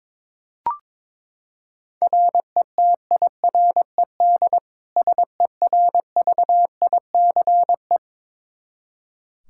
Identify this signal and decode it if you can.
Morse code, heard as a keyed tone, about 22 wpm, 700 Hz. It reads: RETIRED SERVICE
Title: Practice QSO - Exchange 3 - More Info - 22wpm